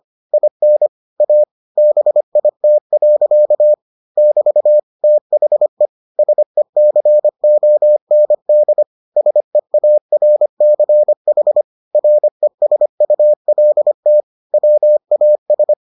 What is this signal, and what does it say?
Morse code, heard as a keyed tone, about 25 wpm, 600 Hz. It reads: IN A BIT. <BT> THE SECOND SEARCH RESULT WAS